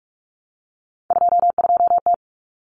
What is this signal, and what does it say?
Morse code, heard as a keyed tone, about 44 wpm, 700 Hz. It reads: 22T
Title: Common Cut Numbers - 3-Digit 45wpm